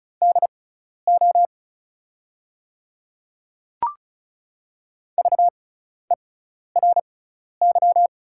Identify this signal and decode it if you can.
Morse code, heard as a keyed tone, about 36 wpm, spaced as for 12 wpm, 690 Hz. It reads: DO VERY
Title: 6x Character Spacing - Top 500 Words - 35wpm